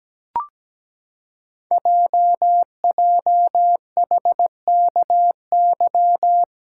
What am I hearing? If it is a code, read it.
Morse code, heard as a keyed tone, about 17 wpm, 700 Hz. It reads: JJHKY
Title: Code Groups - 5 Letters - 17wpm